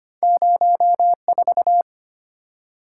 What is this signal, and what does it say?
Morse code, heard as a keyed tone, about 25 wpm, 700 Hz. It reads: 04
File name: Two Letter-Number Combinations - 25wpm